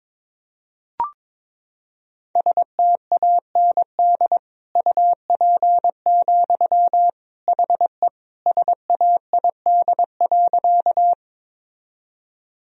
Morse code, heard as a keyed tone, about 22 wpm, 700 Hz. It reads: STAND UP, HE SAID.
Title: Sentences from Top 1000 Words - Review Part 1 - 22wpm